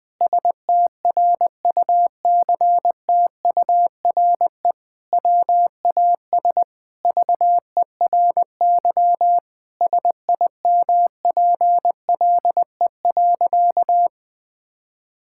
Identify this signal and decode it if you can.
Morse code, heard as a keyed tone, about 21 wpm, 700 Hz. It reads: STRUCTURE WAS VERY SIMPLE.
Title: Sentences from Top 1000 Words - Review Part 6 - 20wpm